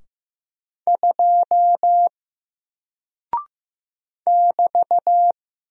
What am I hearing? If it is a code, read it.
Morse code, heard as a keyed tone, about 15 wpm, 700 Hz. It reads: 2 <BT>
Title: Single Character (Letter:Number:Punctuation:Prosign) Alt 15wpm